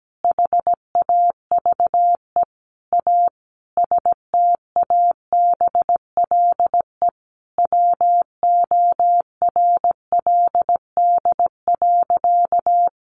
Morse code, heard as a keyed tone, about 17 wpm, 700 Hz. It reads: HAVE A STABLE WORLD.
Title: Sentences from Top 1500 Words 17wpm